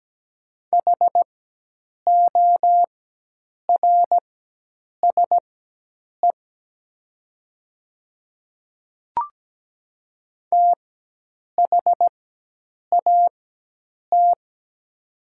Morse code, heard as a keyed tone, about 17 wpm, spaced as for 8 wpm, 700 Hz. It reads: HORSE THAT
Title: 4x Character Spacing - Top 300 Words - 17wpm